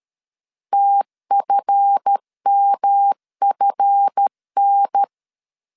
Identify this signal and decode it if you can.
Morse code, heard as a keyed tone, about 13 wpm, 800 Hz. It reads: TFMFN
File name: CW / DET / 1 9 1500C S A 12wpm 3800